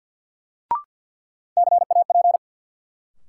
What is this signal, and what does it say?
Morse code, heard as a keyed tone, about 52 wpm, 700 Hz. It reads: CAP
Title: Sentences from Top 1600 Words - New Word Drill Alt - 50wpm